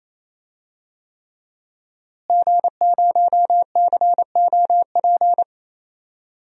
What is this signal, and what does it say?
Morse code, heard as a keyed tone, about 28 wpm, 700 Hz. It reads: G0COP
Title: DXCC-100 Call Signs - 5 Characters - Repeated 3x - 28wpm